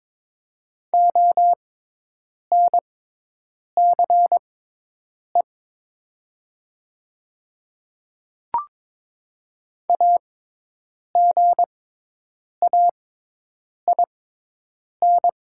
Morse code, heard as a keyed tone, about 22 wpm, spaced as for 8 wpm, 700 Hz. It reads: ONCE AGAIN
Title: 6x Character Spacing - Top 300 Words - 22wpm